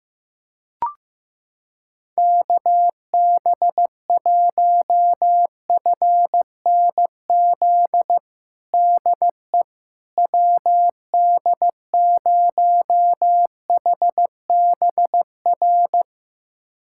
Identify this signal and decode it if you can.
Morse code, heard as a keyed tone, about 15 wpm, 700 Hz. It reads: KB1FNZ DE WD0HBR
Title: USA Callsign DE Callsign - 6 Characters - 15wpm